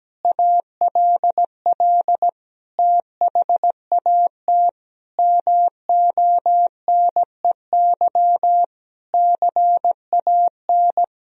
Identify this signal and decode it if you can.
Morse code, heard as a keyed tone, about 17 wpm, 700 Hz. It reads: ALL THAT MONEY CAN
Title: Sentences from Top 1000 Words - Review Part 4 - 17wpm